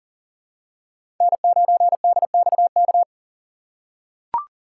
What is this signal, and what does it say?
Morse code, heard as a keyed tone, about 38 wpm, 700 Hz. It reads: N9DXK
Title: USA Call Signs - 5 Characters - 2x Character Spacing - 40wpm